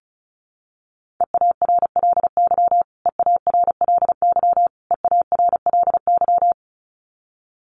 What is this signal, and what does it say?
Morse code, heard as a keyed tone, about 34 wpm, 700 Hz. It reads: EARLY EARLY EARLY
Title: Top 500 Words Repeated 35wpm